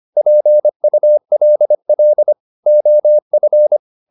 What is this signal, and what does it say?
Morse code, heard as a keyed tone, about 25 wpm, 600 Hz. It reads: PULL OF
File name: the old man and the sea part 1 25wpm